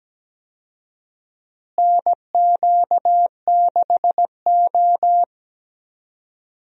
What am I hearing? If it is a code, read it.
Morse code, heard as a keyed tone, about 17 wpm, 700 Hz. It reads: NQ6O
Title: USA Call Signs - 4 Characters - Repeated 3x - 17wpm